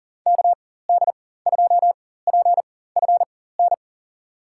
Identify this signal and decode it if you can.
Morse code, heard as a keyed tone, about 40 wpm, spaced as for 19 wpm, 700 Hz. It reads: KD2PFN
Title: USA Call Signs - 4x Character Spacing - 40wpm